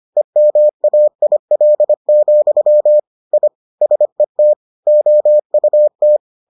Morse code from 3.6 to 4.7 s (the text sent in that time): SET